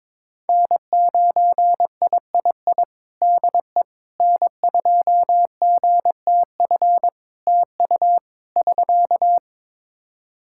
Morse code, heard as a keyed tone, about 22 wpm, 700 Hz. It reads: N9III DE N2GTF TU <SK>
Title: Practice QSO - Exchange 4 - Ending - 22wpm